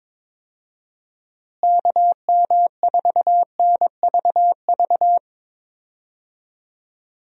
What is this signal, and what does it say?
Morse code, heard as a keyed tone, about 22 wpm, 700 Hz. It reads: KM4NVV
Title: USA Call Signs - 6 Characters - Repeated 2x - 22wpm